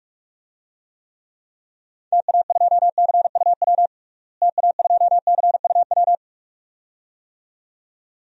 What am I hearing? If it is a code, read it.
Morse code, heard as a keyed tone, about 46 wpm, 700 Hz. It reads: TA2CUW TA2CUW
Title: DXCC-100 Call Signs - 6 Characters - Repeated 2x - 45wpm